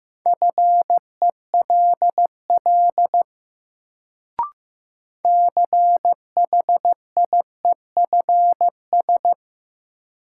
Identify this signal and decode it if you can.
Morse code, heard as a keyed tone, about 15 wpm, 700 Hz. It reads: FELL CHIEFS
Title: Sets of 5 Words Encore - New Word Drill - 15wpm